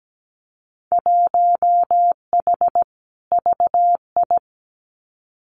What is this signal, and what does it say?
Morse code, heard as a keyed tone, about 17 wpm, 700 Hz. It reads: 1H VI
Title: Winter Field Day Exchanges Encore 17wpm